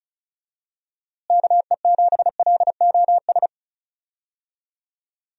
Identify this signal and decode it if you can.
Morse code, heard as a keyed tone, about 34 wpm, 700 Hz. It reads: KE7LOS
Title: USA Call Signs - 6 Characters - Repeated 3x - 35wpm